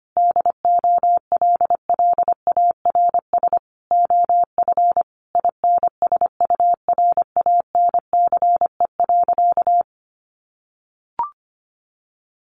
Morse code, heard as a keyed tone, about 25 wpm, 700 Hz. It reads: DOLLARS OF INSURANCE.
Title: A Sound Of Thunder 25wpm